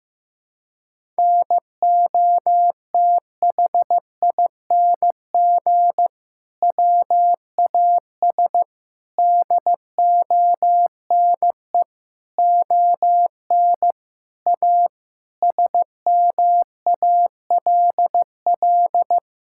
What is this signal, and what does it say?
Morse code, heard as a keyed tone, about 15 wpm, 700 Hz. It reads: NOTHING WAS DONE ON A SMALL
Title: Sentences from Top 1000 Words - Review Part 5 - 15wpm